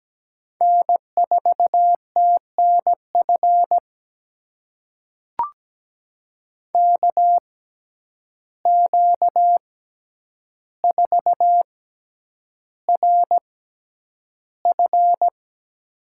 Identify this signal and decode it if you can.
Morse code, heard as a keyed tone, about 17 wpm, spaced as for 9 wpm, 700 Hz. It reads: N4TNF KQ4RF
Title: USA Call Signs - 5 Characters - 6x Character Spacing - 17wpm